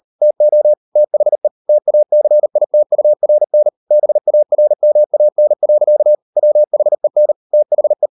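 Morse code, heard as a keyed tone, about 38 wpm, 600 Hz. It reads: TO THE TACITURN BARMAN. WHEN THE
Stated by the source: Morse code practice at 38 WPM, sending Ken Follett's novel Never in English, from CW med Gnister.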